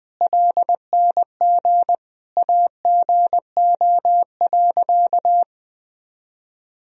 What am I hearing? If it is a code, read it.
Morse code, heard as a keyed tone, about 21 wpm, 700 Hz. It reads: LNG AGO.
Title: Sentences from Top 500 Words and Abbreviations 20wpm